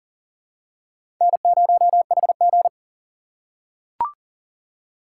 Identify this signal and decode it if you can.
Morse code, heard as a keyed tone, about 40 wpm, 700 Hz. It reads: N0HG
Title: USA Call Signs - 4x Character Spacing - 40wpm